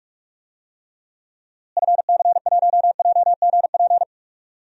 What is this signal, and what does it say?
Morse code, heard as a keyed tone, about 42 wpm, 700 Hz. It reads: FC1JGP
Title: DXCC-100 Call Signs - 6 Characters - Repeated 3x - 45wpm